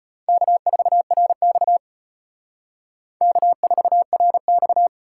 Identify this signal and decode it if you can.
Morse code, heard as a keyed tone, about 31 wpm, 700 Hz. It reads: K4RX K4RX
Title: Speed Racing - USA Call Signs 25wpm